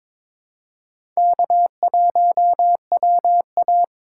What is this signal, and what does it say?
Morse code, heard as a keyed tone, about 22 wpm, 700 Hz. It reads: K1WA
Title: USA Call Signs - 4 Characters - 3x Character Spacing - 22wpm